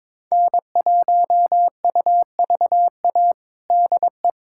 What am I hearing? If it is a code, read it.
Morse code, heard as a keyed tone, about 22 wpm, 700 Hz. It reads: N1UVA DE
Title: USA Callsign DE Callsign - 5 Characters - 22wpm